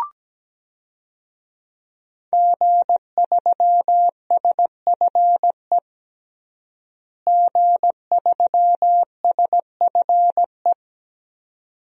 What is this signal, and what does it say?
Morse code, heard as a keyed tone, about 17 wpm, 700 Hz. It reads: G3SFE G3SFE
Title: DXCC-100 Call Signs - 5 Characters - Repeated 2x - 17wpm